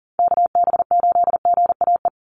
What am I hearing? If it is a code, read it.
Morse code, heard as a keyed tone, about 40 wpm, 700 Hz. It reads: KB8GAE
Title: Call Signs Repeated 2x - 40wpm